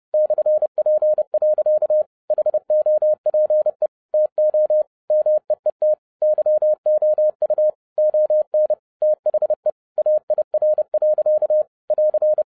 Morse code, heard as a keyed tone, about 30 wpm, 610 Hz. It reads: /P. HOPE TO MEET YOU ON THE AIR. <AR>